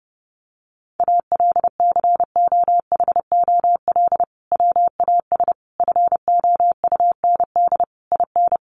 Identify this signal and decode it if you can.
Morse code, heard as a keyed tone, about 30 wpm, 700 Hz. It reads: ALCOHOL WAS FOUND IN
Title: Sentences from Top 1400 Words 30wpm